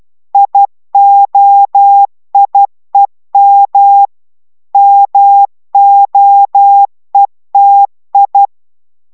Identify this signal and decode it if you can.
Morse code, heard as a keyed tone, about 12 wpm, 800 Hz. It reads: IOIEM MOETI